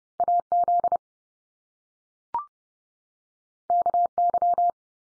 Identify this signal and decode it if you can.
Morse code, heard as a keyed tone, about 30 wpm, 700 Hz. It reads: AZ KY
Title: US States 30WPM